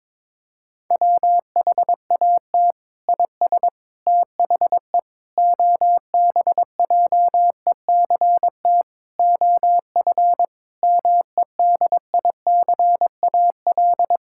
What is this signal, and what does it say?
Morse code, heard as a keyed tone, about 21 wpm, 700 Hz. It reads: WHAT IS THE OBJECT OF MEDICAL
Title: Sentences from Top 1000 Words - Review Part 6 - 22wpm